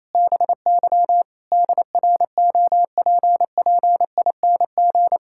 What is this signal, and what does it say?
Morse code, heard as a keyed tone, about 28 wpm, 700 Hz. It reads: BY DROPPING
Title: Pangram Challenge v2 - 28wpm